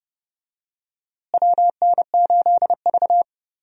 Morse code, heard as a keyed tone, about 30 wpm, 700 Hz. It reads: WN8V
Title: USA Call Signs - 4 Characters - Repeated 2x - 30wpm